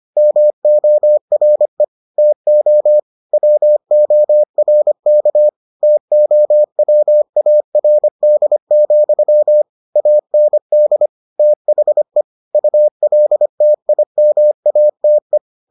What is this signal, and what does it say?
Morse code, heard as a keyed tone, about 25 wpm, 600 Hz. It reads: MORE TO WORK TOWARD, AND THE ULTIMATE